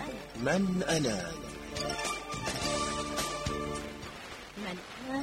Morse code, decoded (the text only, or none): none